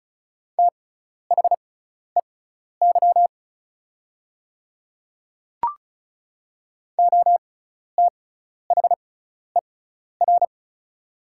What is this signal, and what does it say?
Morse code, heard as a keyed tone, about 36 wpm, spaced as for 12 wpm, 700 Hz. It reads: THEY OTHER
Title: 6x Character Spacing - Top 200 Words - 35wpm